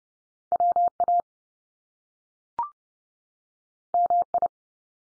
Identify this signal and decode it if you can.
Morse code, heard as a keyed tone, about 30 wpm, 700 Hz. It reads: WA MI
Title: US States 30WPM